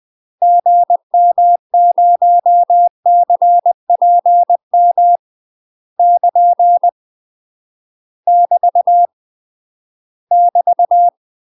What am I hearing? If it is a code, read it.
Morse code, heard as a keyed tone, about 20 wpm, 700 Hz. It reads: GM0CPM <KN> <BT> <BT>
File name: UK MASTER QSOs TRAINING / 20WPM / UK MASTER QSOs 0014